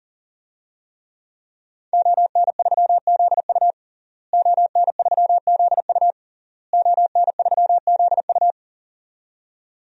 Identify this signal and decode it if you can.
Morse code, heard as a keyed tone, about 40 wpm, 700 Hz. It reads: ON3ZU ON3ZU ON3ZU
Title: DXCC-100 Call Signs - 5 Characters - Repeated 3x - 40wpm